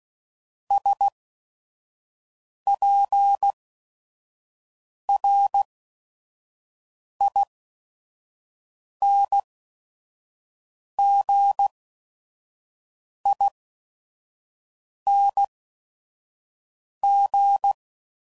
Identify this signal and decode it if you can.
Morse code, heard as a keyed tone, about 16 wpm, spaced as for 5 wpm, 800 Hz. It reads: SPRINGING
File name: Suffixes / Special ING's / VerySlowToFaster ING's